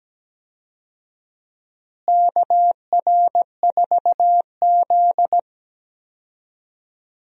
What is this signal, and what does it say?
Morse code, heard as a keyed tone, about 17 wpm, 700 Hz. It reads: KR4Z
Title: USA Call Signs - 4 Characters - Repeated 3x - 17wpm